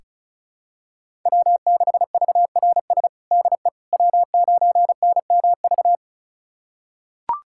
Morse code, heard as a keyed tone, about 34 wpm, 700 Hz. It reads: W6VRS DE W9NMV
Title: USA Callsign DE Callsign - 5 Characters - 35wpm